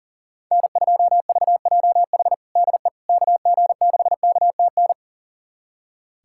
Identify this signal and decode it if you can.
Morse code, heard as a keyed tone, about 40 wpm, 700 Hz. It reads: N2VJH DE KG6KTN
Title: USA Callsign DE Callsign 40wpm